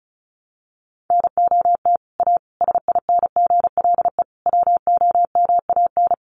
Morse code, heard as a keyed tone, about 34 wpm, 700 Hz. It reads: NOT A SINGLE WOMAN
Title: Sentences from Top 1100 Words 35wpm